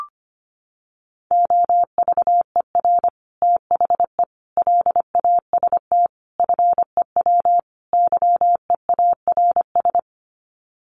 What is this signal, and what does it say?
Morse code, heard as a keyed tone, about 25 wpm, 700 Hz. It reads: OVER THE LAST FEW YEARS
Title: Sets of 5 Words Encore 25wpm